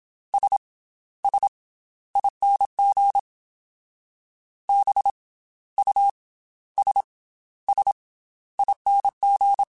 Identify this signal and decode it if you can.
Morse code, heard as a keyed tone, about 27 wpm, 800 Hz. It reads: S S ING B U S S ING
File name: Special INGs / Special INGs-mp3s / Slow ing @25